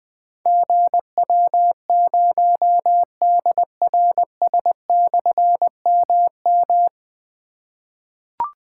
Morse code, heard as a keyed tone, about 20 wpm, 700 Hz. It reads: GW0DRS/MM
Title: Difficult Call Signs 20wpm